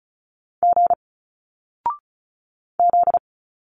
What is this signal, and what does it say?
Morse code, heard as a keyed tone, about 36 wpm, 700 Hz. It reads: G Z
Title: Single Letters 35wpm